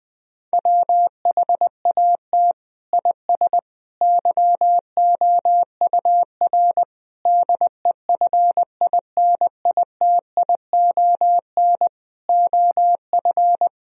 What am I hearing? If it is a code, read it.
Morse code, heard as a keyed tone, about 20 wpm, 700 Hz. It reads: WHAT IS YOUR DEFINITION OF
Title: Sentences from Top 1000 Words - Review Part 4 - 20wpm